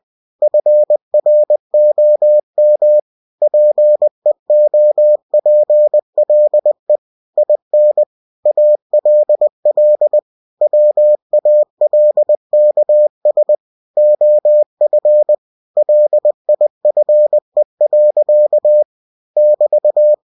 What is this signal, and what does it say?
Morse code, heard as a keyed tone, about 20 wpm, 600 Hz. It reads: FROM PEOPLE IN ALL WALKS OF LIFE. <BT>
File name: cw-books / winnie-the-pooh / winnie-20wpm-page-0120